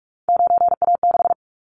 Morse code, heard as a keyed tone, about 44 wpm, 700 Hz. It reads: 9A6
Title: Callsign Prefixes for Top 25 DXCC - 45wpm